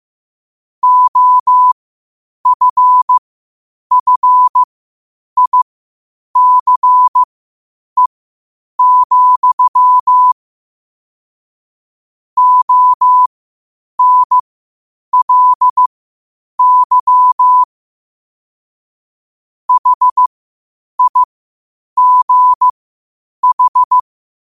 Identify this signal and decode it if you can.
Morse code, heard as a keyed tone, about 15 wpm, spaced as for 8 wpm, 1000 Hz. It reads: OFFICE, ONLY HIGH
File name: qotdfiles / 8wpm